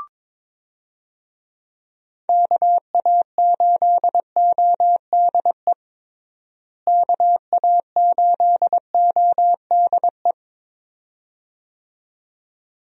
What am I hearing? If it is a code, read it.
Morse code, heard as a keyed tone, about 22 wpm, 700 Hz. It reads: KA8ODE KA8ODE
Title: USA Call Signs - 6 Characters - Repeated 2x - 22wpm